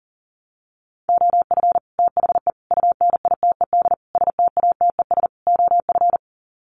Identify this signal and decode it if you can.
Morse code, heard as a keyed tone, about 40 wpm, 700 Hz. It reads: OF THE UNITED STATES OF
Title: Sets of 5 Words Encore 40wpm